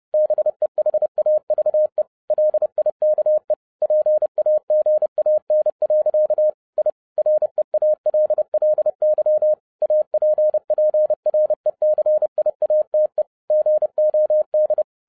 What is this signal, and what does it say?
Morse code, heard as a keyed tone, about 30 wpm, 610 Hz. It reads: BEHAVE LIKE PAGAN. I REALLY APPRECIATE GOD
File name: 2011-07-mp3 / n60 BACK FROM A BIG ACCIDENT 30wpm